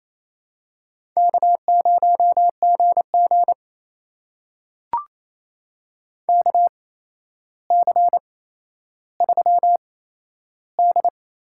Morse code, heard as a keyed tone, about 28 wpm, spaced as for 12 wpm, 700 Hz. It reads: K0GG KC3D
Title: USA Call Signs - 8x Character Spacing - 28wpm